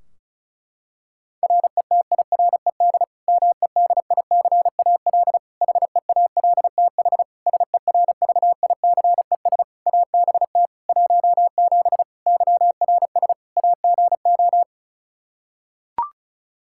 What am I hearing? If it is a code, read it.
Morse code, heard as a keyed tone, about 36 wpm, 700 Hz. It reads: RETIRED MEDICAL HEALTH SERVICES ABT 17 YRS AGO
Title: K1USN SST QSO Practice 35wpm